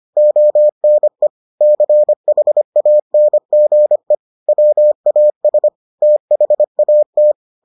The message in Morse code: ONE CHANGE WAS THAT

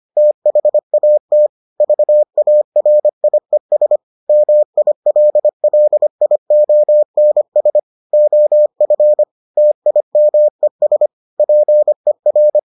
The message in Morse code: THAT VARIES MILLIONS OF TIMES PER

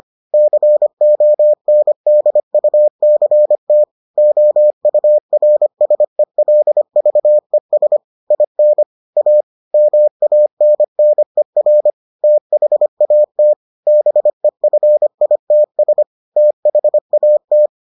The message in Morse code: CONDUCT OURSELVES IN A MANNER THAT BEFITS THAT

